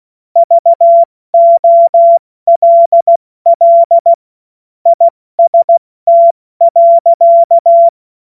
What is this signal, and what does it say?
Morse code, heard as a keyed tone, about 16 wpm, 680 Hz. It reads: VOLL IST.
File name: Morsekurs / Witze / Witz 12 März 2024 70 F80